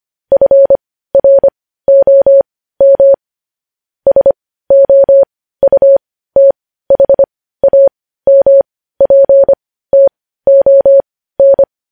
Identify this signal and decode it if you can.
Morse code, heard as a keyed tone, about 25 wpm, spaced as for 15 wpm, 550 Hz. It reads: FROM SOUTHAMPTON
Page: News headlines in morse code at 15 wpm